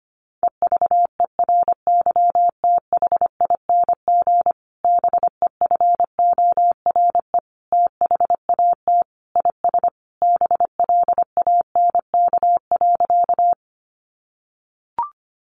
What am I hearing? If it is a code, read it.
Morse code, heard as a keyed tone, about 25 wpm, 700 Hz. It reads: EVERYTHING BEFORE THAT IS BLANK.
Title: Sentences from Top 1500 Words 25wpm